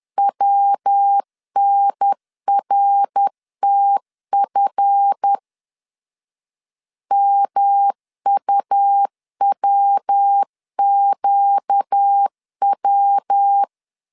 Morse code, heard as a keyed tone, about 10 wpm, 800 Hz. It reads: WNRTF MUWQW
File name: CW / DET / 1 9 1500C S B 10wpm 4439